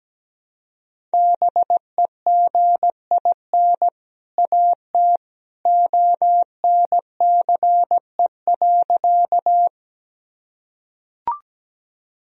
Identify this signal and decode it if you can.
Morse code, heard as a keyed tone, about 17 wpm, 700 Hz. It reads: BEGIN AT ONCE.
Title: Sentences from Top 1000 Words - Review Part 1 - 17wpm